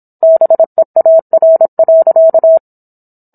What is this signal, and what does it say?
Morse code, heard as a keyed tone, about 27 wpm, 650 Hz. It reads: BEAR.